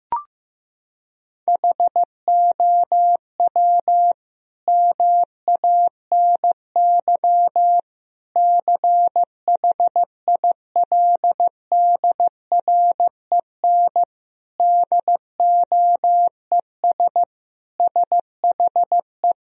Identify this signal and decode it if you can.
Morse code, heard as a keyed tone, about 15 wpm, 700 Hz. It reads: HOW MANY CHILDREN DOES SHE
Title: Sentences from Top 1000 Words - Review Part 1 - 15wpm